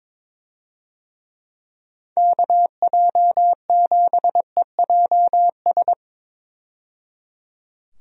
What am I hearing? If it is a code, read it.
Morse code, heard as a keyed tone, about 22 wpm, 700 Hz. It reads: KJ7EJS
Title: USA Call Signs - 6 Characters - 22wpm